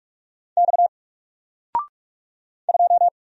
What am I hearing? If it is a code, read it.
Morse code, heard as a keyed tone, about 45 wpm, 700 Hz. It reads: X 2
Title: Single Letter-Number Alt 45wpm